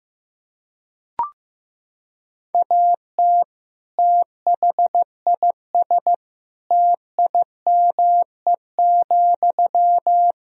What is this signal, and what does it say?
Morse code, heard as a keyed tone, about 15 wpm, 700 Hz. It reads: AT THIS TIME,
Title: Sentences from Top 1000 Words - Review Part 2 - 15wpm